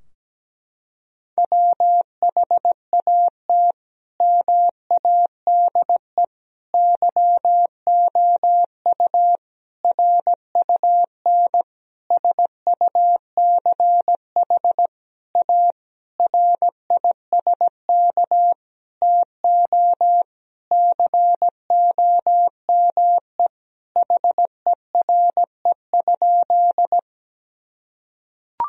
Morse code, heard as a keyed tone, about 17 wpm, 700 Hz. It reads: WHAT MADE YOU RUN SUCH A RISK TO COME HERE?
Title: Sentences from Top 1000 Words - Review Part 4 - 17wpm